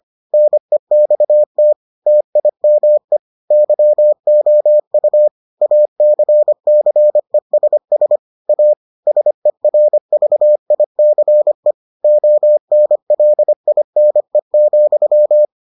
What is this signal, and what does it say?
Morse code, heard as a keyed tone, about 25 wpm, 600 Hz. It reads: NEXT TIME YOU ACCESS A SERVICE ONLINE,